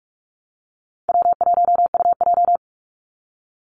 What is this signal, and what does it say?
Morse code, heard as a keyed tone, about 44 wpm, 700 Hz. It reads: W1UJ
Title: Easy Call Signs - 45wpm